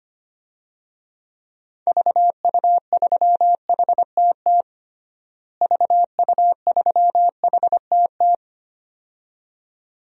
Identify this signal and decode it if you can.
Morse code, heard as a keyed tone, about 25 wpm, 700 Hz. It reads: VU3HTT VU3HTT
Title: DXCC-100 Call Signs - 6 Characters - Repeated 2x - 25wpm